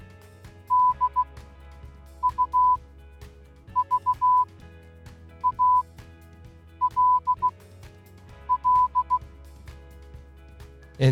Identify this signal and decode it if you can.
Morse code, heard as a keyed tone, about 16 wpm, spaced as for 7 wpm, 1000 Hz, with speech. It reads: DUVALL